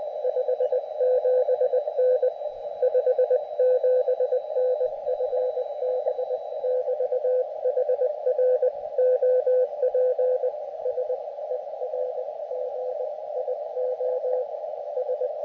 Morse code, heard as a keyed tone, about 20 wpm, 510 Hz, with radio noise. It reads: 57N 57N FB<BT>HR OP SERGIO S
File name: Expedition / SWL UT5NM 20.12.2012 VANGUARD UA1CEG p IT9IFI